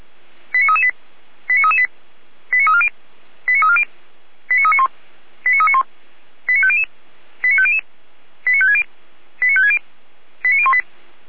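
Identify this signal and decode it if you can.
Five-tone selective call (ZVEI), heard as quick runs of five tones, tones of about 70 ms. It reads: 89338 89338 89340 89340 89391 89391 89500 89500 89560 89560 89018
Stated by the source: ZVEI five-tone selective call received on an SDR.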